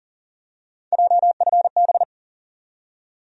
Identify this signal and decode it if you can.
Morse code, heard as a keyed tone, about 40 wpm, 700 Hz. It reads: JFB
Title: USA Call Sign Postfixes - 3 Letters 40wpm